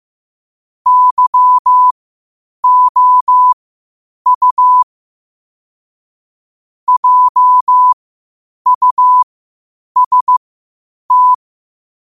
Morse code, heard as a keyed tone, about 15 wpm, spaced as for 8 wpm, 1000 Hz. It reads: YOU JUST
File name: qotdfiles / 8wpm